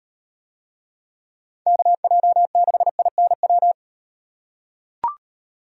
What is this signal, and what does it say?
Morse code, heard as a keyed tone, about 37 wpm, 700 Hz. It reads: KJ6INW